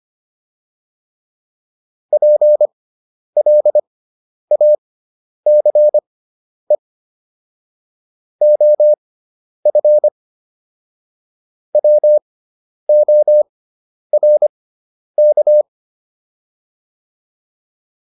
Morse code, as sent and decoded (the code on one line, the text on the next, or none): .--. .-.. .- -.-. / --- ..-. / .-- --- .-. -.-
PLACE OF WORK